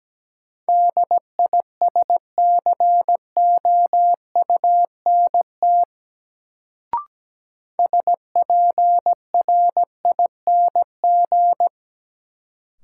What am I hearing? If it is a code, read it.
Morse code, heard as a keyed tone, about 17 wpm, 700 Hz. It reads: DISCOUNT SPRING
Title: Sentences from Top 1200 Words - New Word Drill Alt - 17wpm